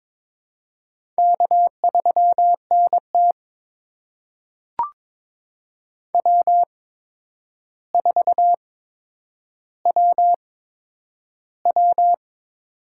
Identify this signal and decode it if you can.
Morse code, heard as a keyed tone, about 22 wpm, spaced as for 9 wpm, 700 Hz. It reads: K3NT W4WW
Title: USA Call Signs - 4 Characters - 8x Character Spacing - 22wpm